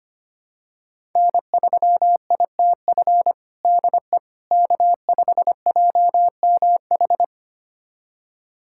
Morse code, heard as a keyed tone, about 25 wpm, 700 Hz. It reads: N3ITF DE K5JMH
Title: USA Callsign DE Callsign - 5 Characters - 25wpm